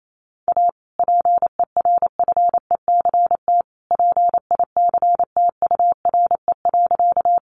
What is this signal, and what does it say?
Morse code, heard as a keyed tone, about 28 wpm, 700 Hz. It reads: A PERFECT PICTURE.
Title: Sentences from Top 1000 Words 28wpm